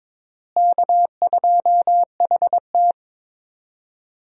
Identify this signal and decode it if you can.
Morse code, heard as a keyed tone, about 21 wpm, 700 Hz. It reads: K2HT
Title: USA Call Signs - 4 Characters - Repeated 3x - 22wpm